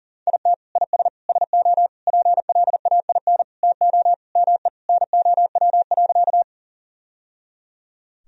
Morse code, heard as a keyed tone, about 40 wpm, 700 Hz. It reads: IT IS SO PLAIN TO ME NOW.
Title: Sentences from Top 1000 Words - Review Part 2 - 40wpm